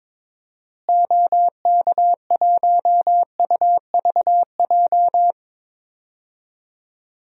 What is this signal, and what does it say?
Morse code, heard as a keyed tone, about 22 wpm, 700 Hz. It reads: OK1UVJ
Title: DXCC-100 Call Signs - 6 Characters - Repeated 2x - 22wpm